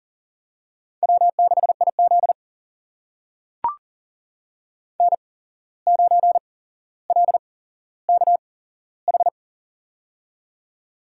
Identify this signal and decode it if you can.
Morse code, heard as a keyed tone, about 40 wpm, spaced as for 16 wpm, 700 Hz. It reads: W6IZ N9LKH
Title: USA Call Signs - 8x Character Spacing - 40wpm